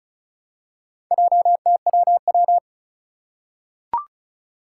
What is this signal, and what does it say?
Morse code, heard as a keyed tone, about 34 wpm, 700 Hz. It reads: JTWW
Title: Four Challenging Letter-Number Combinations 35wpm